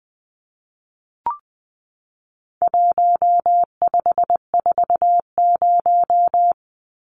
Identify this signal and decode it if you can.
Morse code, heard as a keyed tone, about 20 wpm, 700 Hz. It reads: 1540
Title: Numbers - 4-Digit 20wpm